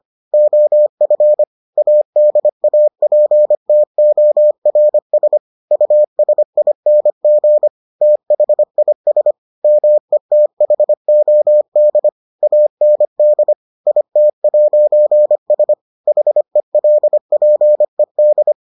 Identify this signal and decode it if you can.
Morse code, heard as a keyed tone, about 25 wpm, 600 Hz. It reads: OF ADAPTORS USING THIS METHOD AND IT'S HELPED